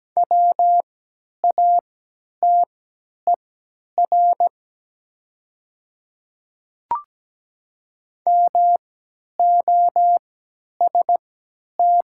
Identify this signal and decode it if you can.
Morse code, heard as a keyed tone, about 17 wpm, spaced as for 10 wpm, 700 Hz. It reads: WATER MOST
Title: 3x Character Spacing - Top 100 Words - 17wpm